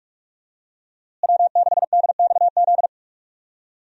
Morse code, heard as a keyed tone, about 42 wpm, 700 Hz. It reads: W6DXZ
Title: USA Call Signs - 5 Characters - Repeated 2x - 45wpm